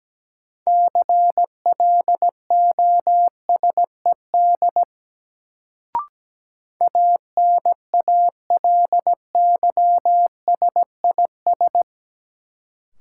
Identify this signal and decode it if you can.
Morse code, heard as a keyed tone, about 17 wpm, 700 Hz. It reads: CLOSED ANALYSIS
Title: Sentences from Top 1400 Words - New Word Drill Alt - 17wpm